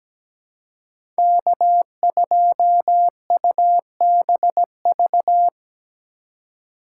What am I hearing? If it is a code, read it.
Morse code, heard as a keyed tone, about 17 wpm, 700 Hz. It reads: K2UBV
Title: USA Call Signs - 5 Characters - 3x Character Spacing - 17wpm